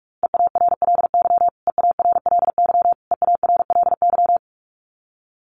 Morse code, heard as a keyed tone, about 44 wpm, 700 Hz. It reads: EARLY EARLY EARLY
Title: Top 500 Words Repeated 45wpm